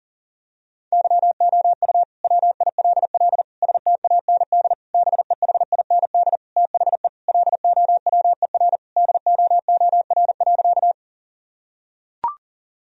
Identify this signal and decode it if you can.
Morse code, heard as a keyed tone, about 40 wpm, 700 Hz. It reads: YOU WILL STAND BEHIND THE LOWER DOOR.
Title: Sentences from Top 1000 Words - Review Part 4 - 40wpm